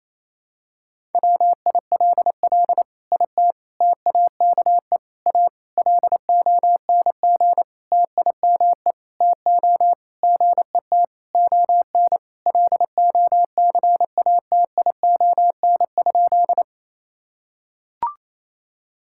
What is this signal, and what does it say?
Morse code, heard as a keyed tone, about 28 wpm, 700 Hz. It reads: WILL IT TAKE A LONG TIME TO GET ON LOCATION?
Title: Sentences from Top 1000 Words - Review Part 6 - 28wpm